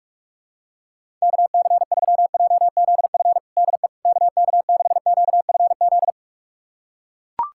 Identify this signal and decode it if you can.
Morse code, heard as a keyed tone, about 41 wpm, 700 Hz. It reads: KC3JZF DE KK6QFZ
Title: USA Callsign DE Callsign 45wpm